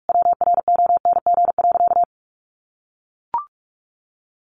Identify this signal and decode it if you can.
Morse code, heard as a keyed tone, about 44 wpm, 700 Hz. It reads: WRONG.
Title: Sentences from Top 900 Words 45wpm